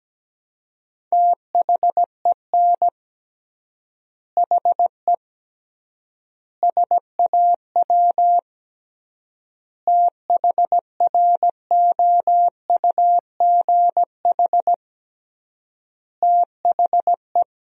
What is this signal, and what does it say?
Morse code, heard as a keyed tone, about 17 wpm, 700 Hz. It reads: THEN HE SAW THROUGH THE
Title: 3x Word-Spacing - Sentences from Top 400 Words - 17wpm